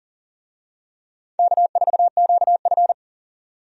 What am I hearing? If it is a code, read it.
Morse code, heard as a keyed tone, about 40 wpm, 700 Hz. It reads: K4QF
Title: USA Call Signs - 4 Characters - 40wpm